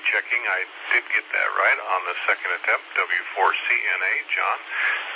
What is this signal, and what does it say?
Emergency Alert System (SAME) tones, heard as harsh buzzing bursts, with speech. It is heard from 1.0 to 4.2 s.